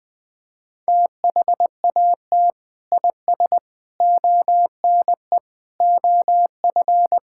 Morse code, heard as a keyed tone, about 20 wpm, 700 Hz. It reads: THAT IS ONE OF
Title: Sentences with 2-Word Phrases - Spoken Beginning - 20wpm